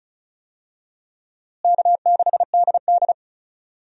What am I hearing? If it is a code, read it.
Morse code, heard as a keyed tone, about 34 wpm, 700 Hz. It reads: K6DD
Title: USA Call Signs - 4 Characters - Repeated 2x - 35wpm